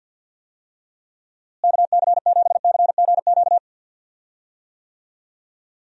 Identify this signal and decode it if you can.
Morse code, heard as a keyed tone, about 47 wpm, 700 Hz. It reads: KC7CGQ
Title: USA Call Signs - 6 Characters - Repeated 2x - 50wpm